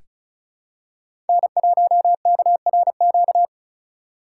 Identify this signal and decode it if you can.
Morse code, heard as a keyed tone, about 34 wpm, 700 Hz. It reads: N1KRQ
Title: USA Call Signs - 2x Character Spacing - 35wpm